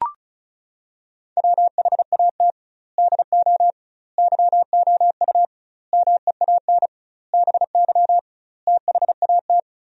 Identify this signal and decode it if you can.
Morse code, heard as a keyed tone, about 36 wpm, 700 Hz. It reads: WHAT DO YOU MEAN BY THAT
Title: 2x Word-Spacing - Sentences from Top 400 Words - 35wpm